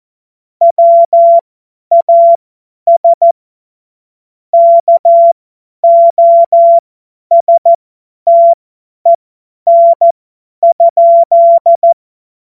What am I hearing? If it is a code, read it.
Morse code, heard as a keyed tone, about 14 wpm, spaced as for 10 wpm, 680 Hz. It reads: WAS KOSTEN?
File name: Morsekurs / Witze / Witz 19. April 2022 50 F70